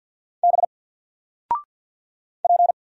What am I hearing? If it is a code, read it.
Morse code, heard as a keyed tone, about 50 wpm, 700 Hz. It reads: B P
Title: Single Letter-Number Alt 50wpm